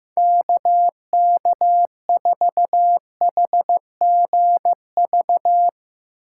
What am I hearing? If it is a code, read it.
Morse code, heard as a keyed tone, about 16 wpm, 700 Hz. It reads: KK4HGV
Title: Speed Racing - USA 2x3 Call Signs 15wpm